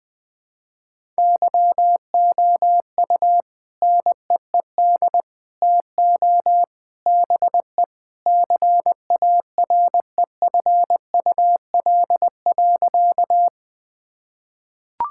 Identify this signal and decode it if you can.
Morse code, heard as a keyed tone, about 20 wpm, 700 Hz. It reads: YOU NEED TO BE CAREFUL.
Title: Sentences with 3-Word Phrases - 20wpm